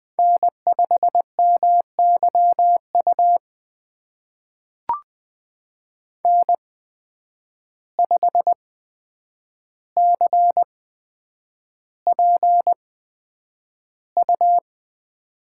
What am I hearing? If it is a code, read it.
Morse code, heard as a keyed tone, about 20 wpm, spaced as for 9 wpm, 700 Hz. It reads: N5MYU N5CPU
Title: USA Call Signs - 5 Characters - 8x Character Spacing - 20wpm